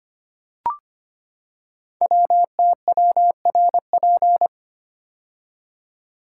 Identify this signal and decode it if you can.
Morse code, heard as a keyed tone, about 25 wpm, 700 Hz. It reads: WTWRP
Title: Code Groups - 5 Letters - 25wpm